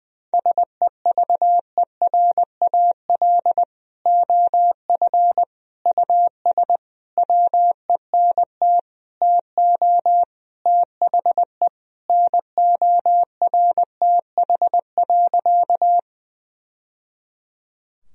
Morse code, heard as a keyed tone, about 21 wpm, 700 Hz. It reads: SEVERAL OF US WENT TO THE NORTH.
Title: Sentences from Top 1000 Words - Review Part 2 - 20wpm